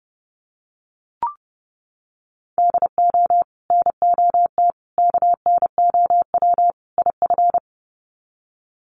Morse code, heard as a keyed tone, about 30 wpm, 700 Hz. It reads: DO NOT KNOW IF
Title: Sets of 4 Words 30wpm